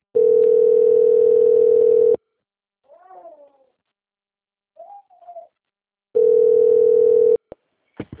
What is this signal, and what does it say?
Ringback tone (North America), 440 + 480 Hz.
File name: out- - - - - .56958